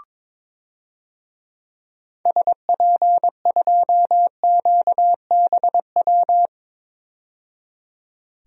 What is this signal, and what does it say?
Morse code, heard as a keyed tone, about 22 wpm, 700 Hz. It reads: SP2QBW
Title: DXCC-100 Call Signs - 6 Characters - 22wpm